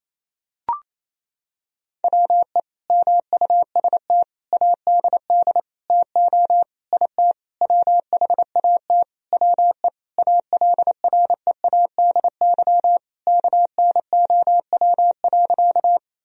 Morse code, heard as a keyed tone, about 28 wpm, 700 Hz. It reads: WE MUST ADD TO IT WHAT WE ALREADY KNOW.
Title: Sentences from Top 1000 Words - Review Part 1 - 28wpm